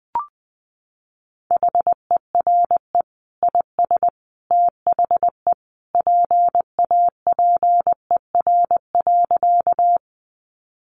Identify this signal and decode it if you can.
Morse code, heard as a keyed tone, about 20 wpm, 700 Hz. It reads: HERE IS THE PAPER.
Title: Sentences from Top 300 Words 20wpm